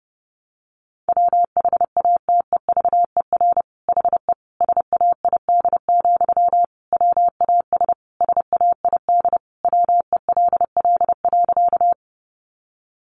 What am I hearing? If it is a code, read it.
Morse code, heard as a keyed tone, about 30 wpm, 700 Hz. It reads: WHATEVER HE SAID, WAS SAID WELL.
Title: Sentences from Top 1100 Words 30wpm